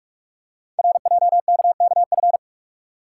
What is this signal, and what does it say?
Morse code, heard as a keyed tone, about 44 wpm, 700 Hz. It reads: RJKKF
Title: Code Groups - 5 Letters - 45wpm